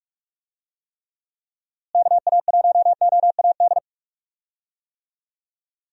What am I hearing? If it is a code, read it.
Morse code, heard as a keyed tone, about 42 wpm, 700 Hz. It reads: KA1OAD
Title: USA Call Signs - 6 Characters - Repeated 3x - 45wpm